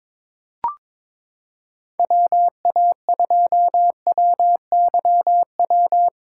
Morse code, heard as a keyed tone, about 21 wpm, 700 Hz. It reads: WA2WYW
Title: USA Callsign DE Callsign - 6 Characters - 22wpm